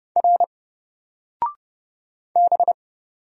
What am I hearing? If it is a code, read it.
Morse code, heard as a keyed tone, about 30 wpm, 700 Hz. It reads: R B
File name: Single-Letter-Number-30wpm